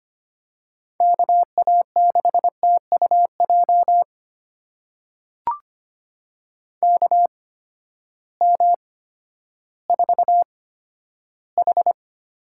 Morse code, heard as a keyed tone, about 25 wpm, spaced as for 12 wpm, 700 Hz. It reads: KA6TUJ KM4H
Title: USA Call Signs - 6 Characters - 8x Character Spacing - 25wpm